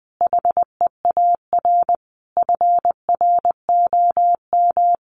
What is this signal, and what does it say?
Morse code, heard as a keyed tone, about 20 wpm, 700 Hz. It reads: HEAR FROM
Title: Common QSO Phrases 20wpm